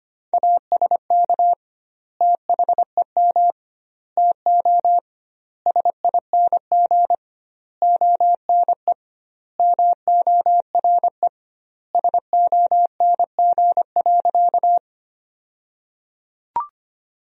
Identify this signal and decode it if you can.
Morse code, heard as a keyed tone, about 25 wpm, 700 Hz. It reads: ASK THEM TO SING ONE MORE SONG.
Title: 2x Word-Spacing - Sentences from Top 400 Words - 25wpm